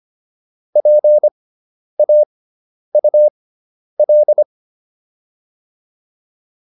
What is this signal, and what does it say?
Morse code, heard as a keyed tone, about 25 wpm, spaced as for 10 wpm, 600 Hz. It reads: PAUL